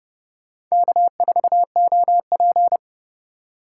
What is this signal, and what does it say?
Morse code, heard as a keyed tone, about 39 wpm, 700 Hz. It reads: K4OP